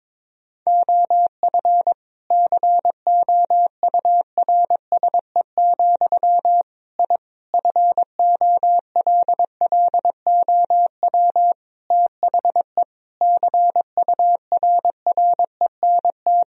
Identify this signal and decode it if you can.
Morse code, heard as a keyed tone, about 21 wpm, 700 Hz. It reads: OF COURSE, I FOLLOW THE CURRENT
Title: Sentences from Top 1000 Words - Review Part 4 - 22wpm